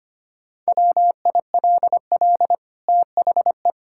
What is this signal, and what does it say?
Morse code, heard as a keyed tone, about 25 wpm, 700 Hz. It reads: WILL THE
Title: Sentences from Top 1000 Words - Review Part 6 - 25wpm